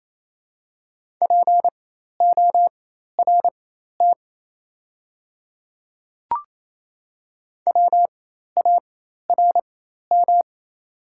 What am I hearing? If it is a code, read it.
Morse code, heard as a keyed tone, about 28 wpm, spaced as for 13 wpm, 700 Hz. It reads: PORT WARM
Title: 4x Character Spacing - Top 500 Words - 28wpm